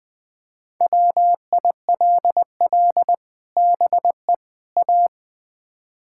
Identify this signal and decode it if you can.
Morse code, heard as a keyed tone, about 20 wpm, 700 Hz. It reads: WILL BE A
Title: Build A Phrase - 3-Word Phrases - 20wpm